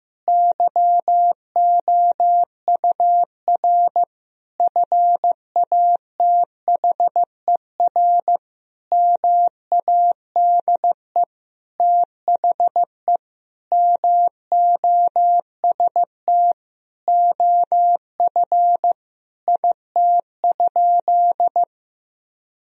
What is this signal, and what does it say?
Morse code, heard as a keyed tone, about 15 wpm, 700 Hz. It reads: YOUR FATHER MADE THE MOST OF IT?
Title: Sentences from Top 1000 Words - Review Part 1 - 15wpm